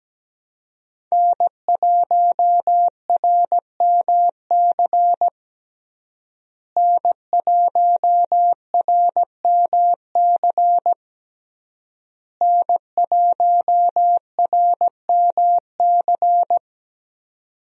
Morse code, heard as a keyed tone, about 17 wpm, 700 Hz. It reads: N1RMC N1RMC N1RMC
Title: USA Call Signs - 5 Characters - Repeated 3x - 17wpm